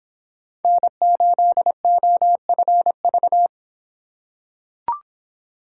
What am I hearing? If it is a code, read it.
Morse code, heard as a keyed tone, about 25 wpm, 700 Hz. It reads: N8OFV